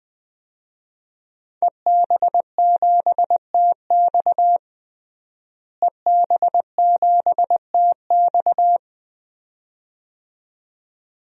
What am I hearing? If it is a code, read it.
Morse code, heard as a keyed tone, about 20 wpm, 700 Hz. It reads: EB7TX EB7TX
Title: DXCC-100 Call Signs - 5 Characters - Repeated 2x - 20wpm